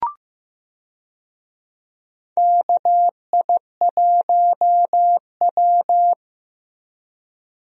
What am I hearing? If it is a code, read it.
Morse code, heard as a keyed tone, about 15 wpm, 700 Hz. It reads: KI1W
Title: USA Call Signs - 4 Characters - Repeated 3x - 15wpm